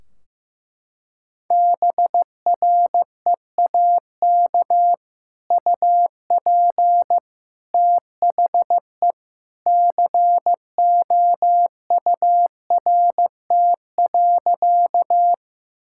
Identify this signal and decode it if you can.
Morse code, heard as a keyed tone, about 15 wpm, 700 Hz. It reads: BREAK UP THE COURT.
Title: Sentences from Top 1600 Words 15wpm